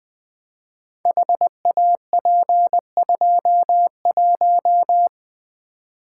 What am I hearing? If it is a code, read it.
Morse code, heard as a keyed tone, about 20 wpm, 700 Hz. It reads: HAP21
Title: Code Groups - 5 Letter:Numbers - 20wpm